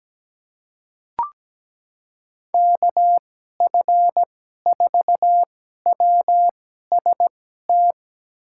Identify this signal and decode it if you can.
Morse code, heard as a keyed tone, about 17 wpm, spaced as for 12 wpm, 700 Hz. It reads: KF4WST
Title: USA Call Signs - 6 Characters - 2x Character Spacing - 17wpm